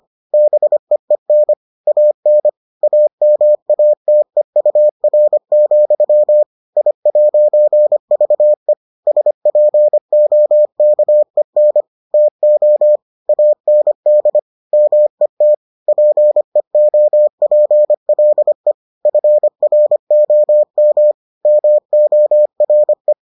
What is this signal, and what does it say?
Morse code, heard as a keyed tone, about 25 wpm, 600 Hz. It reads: BEEN AN AMATEUR, I'VE SPOKEN TO AND MET PEOPLE FROM MORE